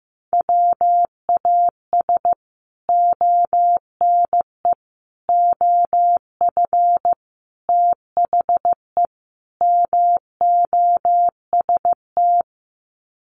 Morse code, heard as a keyed tone, about 15 wpm, 700 Hz. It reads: WAS ONE OF THE MOST
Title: Sets of 5 Words 15wpm